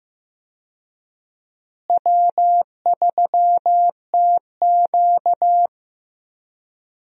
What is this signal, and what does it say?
Morse code, heard as a keyed tone, about 15 wpm, 700 Hz. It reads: W3TQ
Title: USA Call Signs - 4 Characters - Repeated 2x - 15wpm